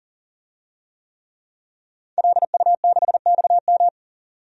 Morse code, heard as a keyed tone, about 40 wpm, 700 Hz. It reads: LU6XM
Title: DXCC-100 Call Signs - 5 Characters - 40wpm